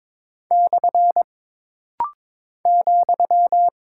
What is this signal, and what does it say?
Morse code, heard as a keyed tone, about 22 wpm, 700 Hz. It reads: / ,
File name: practice / 22wpm / Single Punctuation-Prosign - Rapid-Fire - 22wpm